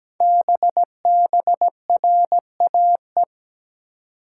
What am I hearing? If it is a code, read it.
Morse code, heard as a keyed tone, about 17 wpm, 700 Hz. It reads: BBRAE
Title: Code Groups - 5 Letters - 17wpm